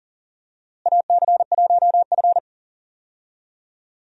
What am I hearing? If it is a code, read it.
Morse code, heard as a keyed tone, about 40 wpm, 700 Hz. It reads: AC1F
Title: USA Call Signs - 4 Characters - Repeated 2x - 40wpm